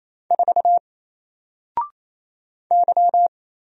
Morse code, heard as a keyed tone, about 27 wpm, 700 Hz. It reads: 4 Y
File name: Single-Letter-Number-28wpm